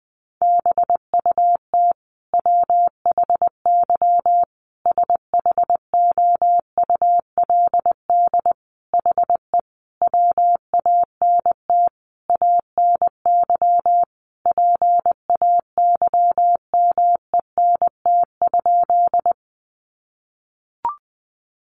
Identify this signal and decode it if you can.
Morse code, heard as a keyed tone, about 20 wpm, 700 Hz. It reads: BUT WHY SHOULD HE WANT ANY PAYMENT?
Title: Sentences from Top 1200 Words 20wpm